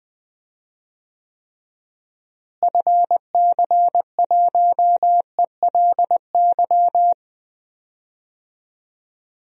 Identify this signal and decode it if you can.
Morse code, heard as a keyed tone, about 19 wpm, 700 Hz. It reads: FC1ELY
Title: DXCC-100 Call Signs - 6 Characters - Repeated 2x - 20wpm